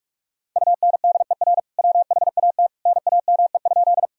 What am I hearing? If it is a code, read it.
Morse code, heard as a keyed tone, about 47 wpm, 700 Hz. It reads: UNDER WHAT NAME?
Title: Sentences from Top 1000 Words - Review Part 1 - 45wpm